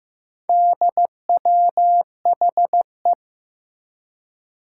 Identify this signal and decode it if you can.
Morse code, heard as a keyed tone, about 15 wpm, 700 Hz. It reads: DWHE
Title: Four Challenging Letter-Number Combinations 15wpm